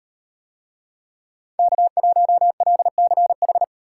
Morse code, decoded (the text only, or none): K1LCH